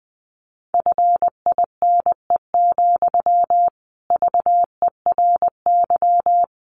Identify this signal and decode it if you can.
Morse code, heard as a keyed tone, about 20 wpm, 700 Hz. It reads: FINE, VERY
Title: Sentences from Top 500 Words 20wpm